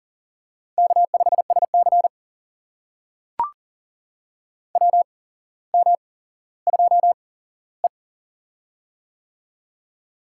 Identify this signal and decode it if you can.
Morse code, heard as a keyed tone, about 40 wpm, spaced as for 17 wpm, 700 Hz. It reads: K5SC WM2E
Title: USA Call Signs - 4 Characters - 8x Character Spacing - 40wpm